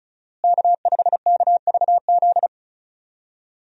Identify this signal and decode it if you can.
Morse code, heard as a keyed tone, about 36 wpm, 700 Hz. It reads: K5KVZ
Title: USA Call Signs - 5 Characters - Repeated 3x - 35wpm